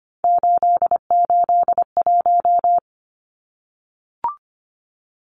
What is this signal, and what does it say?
Morse code, heard as a keyed tone, about 25 wpm, 700 Hz. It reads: 881
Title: Numbers - 3-Digit 25wpm